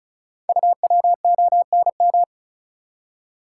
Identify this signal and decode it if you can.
Morse code, heard as a keyed tone, about 36 wpm, 690 Hz. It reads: UWONM
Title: Code Groups - 5 Letters - 35wpm